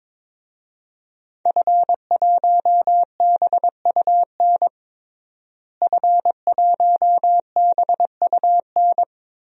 Morse code, heard as a keyed tone, about 22 wpm, 700 Hz. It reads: F1BUN F1BUN
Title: DXCC-100 Call Signs - 5 Characters - Repeated 3x - 22wpm